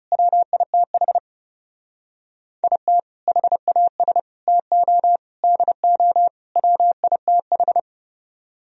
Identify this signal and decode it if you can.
Morse code, heard as a keyed tone, about 36 wpm, 700 Hz. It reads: WITH IT HAS TO DO WITH
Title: Speed-Racing - Sets of 5 Words 30wpm